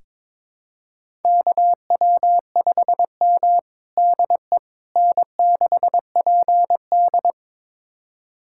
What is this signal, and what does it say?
Morse code, heard as a keyed tone, about 22 wpm, 700 Hz. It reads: KW5M DE N6PD
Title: USA Callsign DE Callsign - 4 Characters - 22wpm